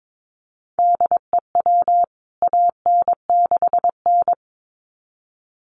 Morse code, heard as a keyed tone, about 22 wpm, 700 Hz. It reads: DEW AN6N
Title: CWT Exchange 22wpm